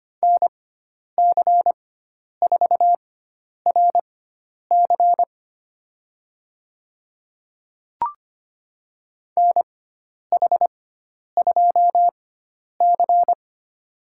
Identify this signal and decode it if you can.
Morse code, heard as a keyed tone, about 25 wpm, spaced as for 10 wpm, 700 Hz. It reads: NC4RC NH2C
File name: course / 25:10wpm / 095 Lesson 16g - Letter c - Call Signs 25wpm